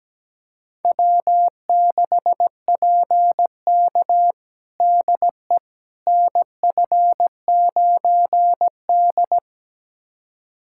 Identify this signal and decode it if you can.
Morse code, heard as a keyed tone, about 17 wpm, 700 Hz. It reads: W6PK DE NF9D
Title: USA Callsign DE Callsign - 4 Characters - 17wpm